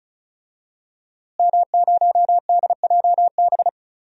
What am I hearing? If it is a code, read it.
Morse code, heard as a keyed tone, about 36 wpm, 690 Hz. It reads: M0DJB
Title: DXCC-100 Call Signs - 5 Characters - Repeated 3x - 35wpm